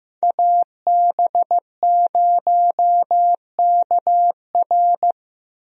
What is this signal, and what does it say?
Morse code, heard as a keyed tone, about 15 wpm, 700 Hz. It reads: AB0KR
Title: USA Call Signs - 5 Characters - Repeated 2x - 15wpm